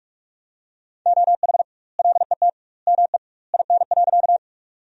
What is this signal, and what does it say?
Morse code, heard as a keyed tone, about 41 wpm, 700 Hz. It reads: OH LET ME IN.
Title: 2x Word-Spacing - Sentences from Top 500 Words - 45wpm